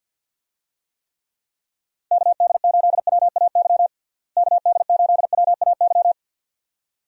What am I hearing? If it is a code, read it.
Morse code, heard as a keyed tone, about 50 wpm, 700 Hz. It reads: KD8WAY KD8WAY
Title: USA Call Signs - 6 Characters - Repeated 2x - 50wpm